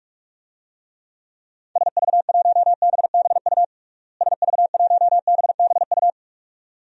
Morse code, heard as a keyed tone, about 46 wpm, 700 Hz. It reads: SV1BBU SV1BBU
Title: DXCC-100 Call Signs - 6 Characters - Repeated 2x - 45wpm